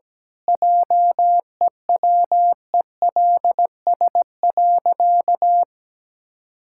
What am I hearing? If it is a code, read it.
Morse code, heard as a keyed tone, about 17 wpm, 700 Hz. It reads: JEWELS.
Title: Pangram Challenge v2 - 17wpm